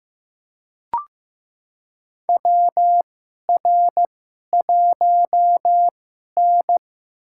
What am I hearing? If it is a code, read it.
Morse code, heard as a keyed tone, about 15 wpm, spaced as for 11 wpm, 700 Hz. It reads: WR1N
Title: USA Call Signs - 4 Characters - 2x Character Spacing - 15wpm